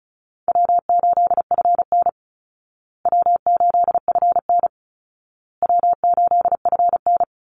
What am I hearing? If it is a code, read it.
Morse code, heard as a keyed tone, about 34 wpm, 700 Hz. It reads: W8FN W8FN W8FN
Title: Easy Call Signs Repeated 3x - 35wpm